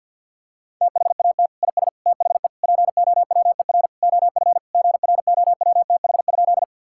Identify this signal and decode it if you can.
Morse code, heard as a keyed tone, about 50 wpm, 700 Hz. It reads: THAT IS THE POWER OF GROWTH?
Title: Sentences from Top 1000 Words - Review Part 5 - 50wpm